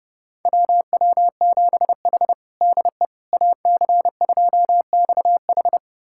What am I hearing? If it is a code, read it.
Morse code, heard as a keyed tone, about 30 wpm, 700 Hz. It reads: WW7H DE AC2XH
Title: USA Callsign DE Callsign 30wpm